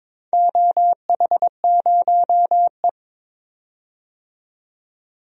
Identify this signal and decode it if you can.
Morse code, heard as a keyed tone, about 22 wpm, 700 Hz. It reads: OH0E
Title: DXCC-100 Call Signs - 4 Characters - 22wpm